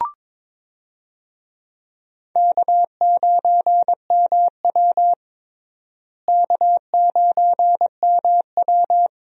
Morse code, heard as a keyed tone, about 21 wpm, 700 Hz. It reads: K9MW K9MW
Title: USA Call Signs - 4 Characters - Repeated 2x - 22wpm